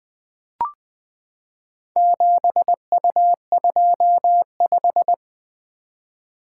Morse code, heard as a keyed tone, about 20 wpm, 700 Hz. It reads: 7U25
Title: Four Letter-Number Combinations - 20wpm